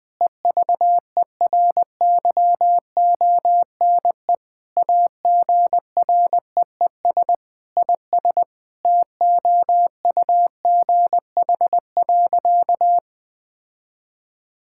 Morse code, heard as a keyed tone, about 20 wpm, 700 Hz. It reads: EVERYONE AGREES IS TOUGH.
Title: Sentences with 7-Word Phrases - Spoken Beginning - 20wpm